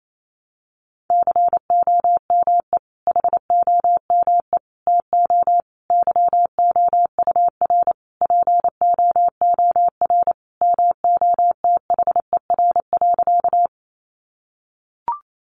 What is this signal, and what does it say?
Morse code, heard as a keyed tone, about 28 wpm, 700 Hz. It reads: COME HOME TO YOUR POOR MOTHER.
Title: Sentences from Top 1000 Words 28wpm